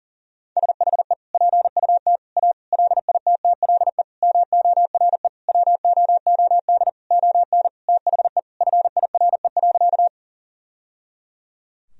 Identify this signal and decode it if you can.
Morse code, heard as a keyed tone, about 40 wpm, 700 Hz. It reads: SHE PUT A LITTLE MORE WOOD ON THE FIRE.
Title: Sentences from Top 1000 Words - Review Part 2 - 40wpm